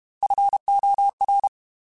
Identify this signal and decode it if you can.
Morse code, heard as a keyed tone, about 32 wpm, 800 Hz. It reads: FOR
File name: Technical Subjects / Technical Subjects-mp3s / AND Gate @30